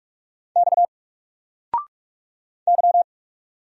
Morse code, heard as a keyed tone, about 44 wpm, 700 Hz. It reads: X Y
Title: Single Character (Letter:Number:Punctuation:Prosign) Alt 45wpm